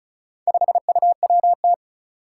Morse code, heard as a keyed tone, about 36 wpm, 700 Hz. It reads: 5UWT
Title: Four Challenging Letter-Number Combinations 35wpm